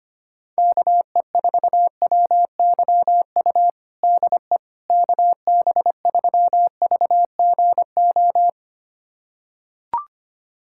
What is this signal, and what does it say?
Morse code, heard as a keyed tone, about 25 wpm, 700 Hz. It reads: KE4WYU DE KB3VGO
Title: USA Callsign DE Callsign - 6 Characters - 25wpm